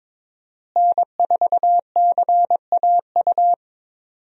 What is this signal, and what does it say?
Morse code, heard as a keyed tone, about 22 wpm, 700 Hz. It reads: N4CAU
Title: USA Call Signs - 5 Characters - Repeated 2x - 22wpm